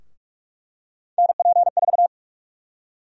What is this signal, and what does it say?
Morse code, heard as a keyed tone, about 46 wpm, 700 Hz. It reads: NP4
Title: USA Call Sign Prefixes - 45wpm